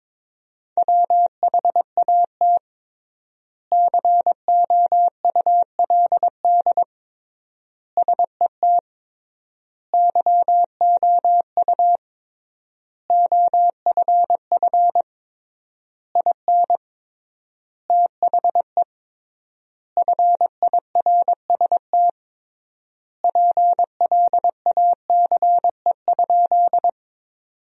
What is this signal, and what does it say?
Morse code, heard as a keyed tone, about 22 wpm, 700 Hz. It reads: WHAT COULD SET YOU OFF IN THE FIRST PLACE?
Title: 3x Word-Spacing - Sentences from Top 200 Words - 22wpm